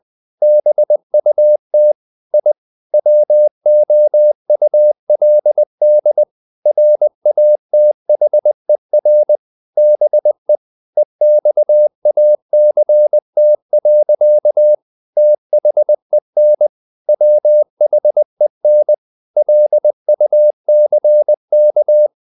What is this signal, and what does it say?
Morse code, heard as a keyed tone, about 20 wpm, 600 Hz. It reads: BUT I WOULD RATHER BE EXACT. THEN WHEN LUCK
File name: the old man and the sea / the old man and the sea part 1 20wpm